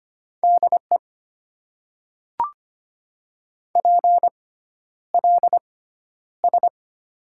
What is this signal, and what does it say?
Morse code, heard as a keyed tone, about 25 wpm, spaced as for 11 wpm, 700 Hz. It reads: DE PLS
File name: Top 90 QSO Elements - 6x Character Spacing - 25wpm